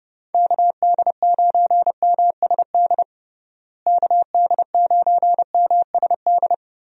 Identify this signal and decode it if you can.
Morse code, heard as a keyed tone, about 30 wpm, 700 Hz. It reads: KD9MSD KD9MSD
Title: USA Call Signs - 6 Characters - Repeated 2x - 30wpm